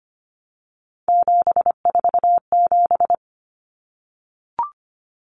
Morse code, heard as a keyed tone, about 25 wpm, 700 Hz. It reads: 747
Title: Numbers - 3-Digit 25wpm